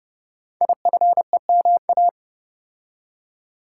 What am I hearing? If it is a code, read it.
Morse code, heard as a keyed tone, about 28 wpm, 700 Hz. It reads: IFEMA
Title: Code Groups - 5 Letter:Numbers - 30wpm